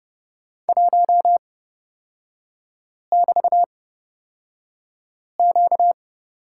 Morse code, heard as a keyed tone, about 30 wpm, spaced as for 5 wpm, 700 Hz. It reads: 1<BT>Q
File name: Lesson 38e - Letter q - Three Characters 30wpm